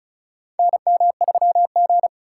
Morse code, heard as a keyed tone, about 36 wpm, 700 Hz. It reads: NM3G
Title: USA Call Signs - 4 Characters - 2x Character Spacing - 35wpm